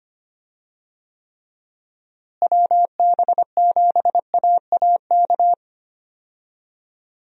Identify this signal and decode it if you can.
Morse code, heard as a keyed tone, about 25 wpm, 700 Hz. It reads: WB7AAK
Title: USA Call Signs - 6 Characters - Repeated 2x - 25wpm